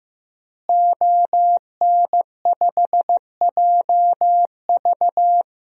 Morse code, heard as a keyed tone, about 16 wpm, 700 Hz. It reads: ON5JV
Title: DXCC-100 Call Signs - 5 Characters - Repeated 2x - 15wpm